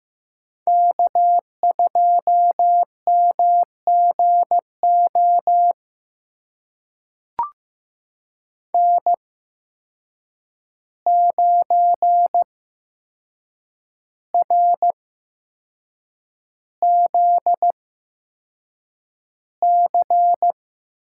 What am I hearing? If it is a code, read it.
Morse code, heard as a keyed tone, about 15 wpm, spaced as for 6 wpm, 700 Hz. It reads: K2MGO N9RZC
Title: USA Call Signs - 5 Characters - 8x Character Spacing - 15wpm